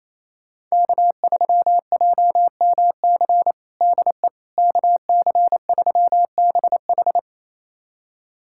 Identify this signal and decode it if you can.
Morse code, heard as a keyed tone, about 27 wpm, 700 Hz. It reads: K3JMC DE KC3BH
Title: USA Callsign DE Callsign - 5 Characters - 28wpm